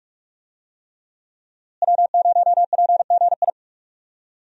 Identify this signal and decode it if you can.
Morse code, heard as a keyed tone, about 46 wpm, 700 Hz. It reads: W0PGI
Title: USA Call Signs - 5 Characters - Repeated 3x - 45wpm